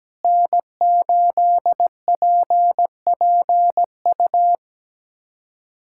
Morse code, heard as a keyed tone, about 17 wpm, 700 Hz. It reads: N8PPU
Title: USA Callsign DE Callsign - 5 Characters - 17wpm